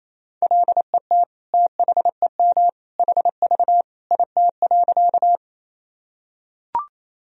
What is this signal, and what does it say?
Morse code, heard as a keyed tone, about 28 wpm, 700 Hz. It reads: LET THEM HV IT.
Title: Sentences from Top 300 Words and Abbreviations 28wpm